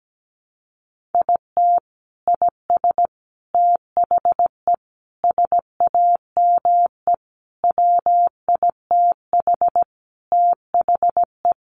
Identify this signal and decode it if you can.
Morse code, heard as a keyed tone, about 17 wpm, 700 Hz. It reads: IT IS THE SAME WITH THE
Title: Sentences from Top 1100 Words 17wpm